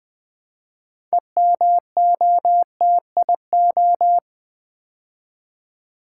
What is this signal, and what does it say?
Morse code, heard as a keyed tone, about 20 wpm, 700 Hz. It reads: EMOTIO
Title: Build A Word - 3 Letters At A Time - 9-Letter Words - 20wpm